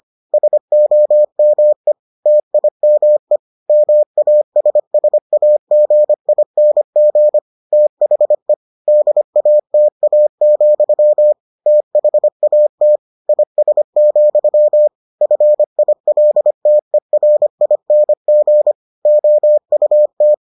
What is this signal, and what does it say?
Morse code, heard as a keyed tone, about 25 wpm, 600 Hz. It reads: SOME TIME MASSAGING THE DATA, THAT IS, FILTERING OUT